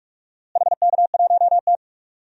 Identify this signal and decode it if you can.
Morse code, heard as a keyed tone, about 45 wpm, 700 Hz. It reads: HK1T
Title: DXCC-100 Call Signs - 4 Characters - Repeated 2x - 45wpm